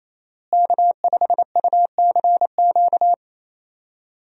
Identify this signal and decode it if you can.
Morse code, heard as a keyed tone, about 28 wpm, 700 Hz. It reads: K5UCQ
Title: USA Call Signs - 5 Characters - Repeated 3x - 28wpm